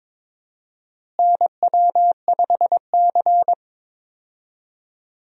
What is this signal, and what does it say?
Morse code, heard as a keyed tone, about 21 wpm, 700 Hz. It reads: NW5C
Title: USA Call Signs - 4 Characters - Repeated 3x - 22wpm